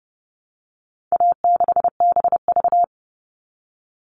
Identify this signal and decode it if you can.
Morse code, heard as a keyed tone, about 30 wpm, 700 Hz. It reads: A6BV
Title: All Cut Numbers - 4-Digit 30wpm